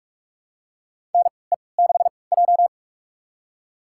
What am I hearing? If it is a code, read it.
Morse code, heard as a keyed tone, about 45 wpm, spaced as for 25 wpm, 700 Hz. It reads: NE6J
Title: USA Call Signs - 4 Characters - 3x Character Spacing - 45wpm